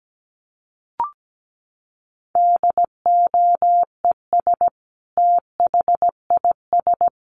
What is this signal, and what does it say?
Morse code, heard as a keyed tone, about 17 wpm, 700 Hz. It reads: DOES THIS
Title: Sentences from Top 1500 Words 17wpm